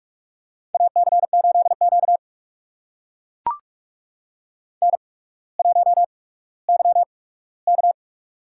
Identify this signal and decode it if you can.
Morse code, heard as a keyed tone, about 44 wpm, spaced as for 19 wpm, 700 Hz. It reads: AC8Q N1YK
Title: USA Call Signs - 8x Character Spacing - 45wpm